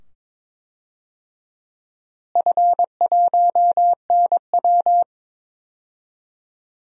Morse code, heard as a keyed tone, about 22 wpm, 700 Hz. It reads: F1NW
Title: DXCC-100 Call Signs - 4 Characters - Repeated 2x - 22wpm